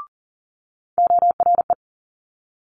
Morse code, heard as a keyed tone, about 41 wpm, 700 Hz. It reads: ORE
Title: Multi-letter Phonemes Quick Fire 40wpm